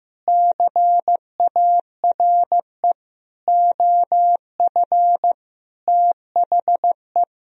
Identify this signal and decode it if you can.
Morse code, heard as a keyed tone, about 15 wpm, 700 Hz. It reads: CARE OF THE
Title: Sentences from Top 400 Words and Abbreviations 15wpm